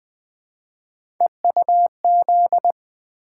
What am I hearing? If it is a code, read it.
Morse code, heard as a keyed tone, about 20 wpm, 700 Hz. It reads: EUZ
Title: Three Letter-Number Combinations - 20wpm